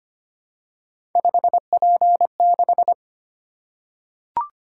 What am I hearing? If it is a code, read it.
Morse code, heard as a keyed tone, about 25 wpm, 700 Hz. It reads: 5P6
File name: Three Letter-Number Combinations - 25wpm